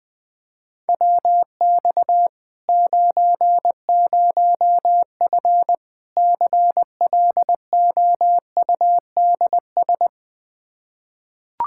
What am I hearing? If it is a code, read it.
Morse code, heard as a keyed tone, about 21 wpm, 700 Hz. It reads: WX 90F CLOUDS
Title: Practice QSO - Exchange 2 - Info - 20wpm